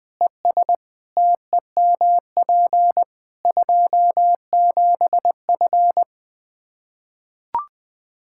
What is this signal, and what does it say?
Morse code, heard as a keyed tone, about 20 wpm, 700 Hz. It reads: ES TEMP 27F
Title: Practice QSO - Exchange 2 - Info - 20wpm